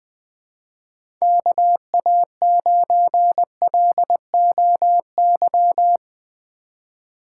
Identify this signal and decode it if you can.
Morse code, heard as a keyed tone, about 20 wpm, 700 Hz. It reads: KA9LOY
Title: USA Call Signs - 6 Characters - 4x Character Spacing - 20wpm